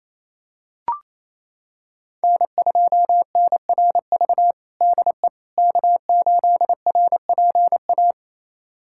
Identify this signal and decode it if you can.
Morse code, heard as a keyed tone, about 27 wpm, 700 Hz. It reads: N2NRV DE K8RPA
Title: USA Callsign DE Callsign - 5 Characters - 28wpm